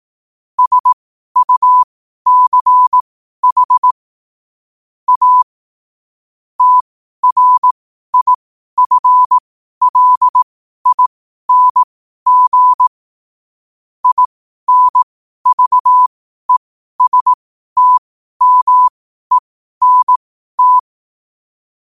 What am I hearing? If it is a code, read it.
Morse code, heard as a keyed tone, about 18 wpm, spaced as for 12 wpm, 1000 Hz. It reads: SUCH A TRIFLING INVESTMENT